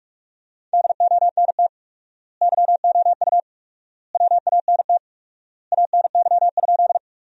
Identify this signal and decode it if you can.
Morse code, heard as a keyed tone, about 46 wpm, 700 Hz. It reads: DONT YOU WANT ANY?
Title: 4x Word-Spacing - Sentences from Top 300 Words - 45wpm